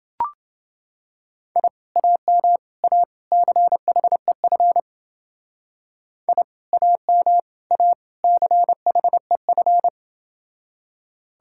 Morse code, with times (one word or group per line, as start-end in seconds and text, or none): 1.6-1.7 I
2.0-2.6 AM
2.8-3.0 A
3.3-4.8 CHEF
6.3-6.4 I
6.7-7.4 AM
7.7-7.9 A
8.2-9.9 CHEF